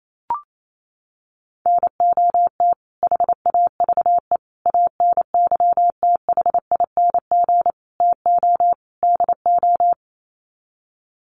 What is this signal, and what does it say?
Morse code, heard as a keyed tone, about 28 wpm, 700 Hz. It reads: NOT HAVE ANYTHING TO DO
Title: Sets of 5 Words Encore 28wpm